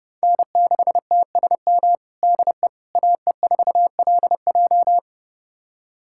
Morse code, heard as a keyed tone, about 30 wpm, 700 Hz. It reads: N6TSM DE AE4LJ
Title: USA Callsign DE Callsign - 5 Characters - 30wpm